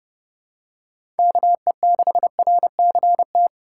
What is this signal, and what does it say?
Morse code, heard as a keyed tone, about 30 wpm, 700 Hz. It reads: KE6RCT
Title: USA Call Signs - 6 Characters - 4x Character Spacing - 30wpm